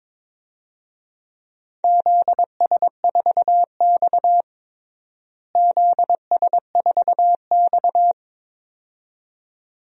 Morse code, heard as a keyed tone, about 22 wpm, 700 Hz. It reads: ZS4X ZS4X
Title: DXCC-100 Call Signs - 4 Characters - Repeated 2x - 22wpm